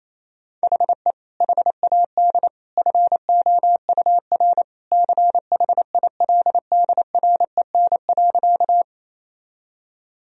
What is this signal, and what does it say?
Morse code, heard as a keyed tone, about 28 wpm, 700 Hz. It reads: HE HAD FOUR CHILDREN.
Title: Sentences from Top 1000 Words - Review Part 1 - 28wpm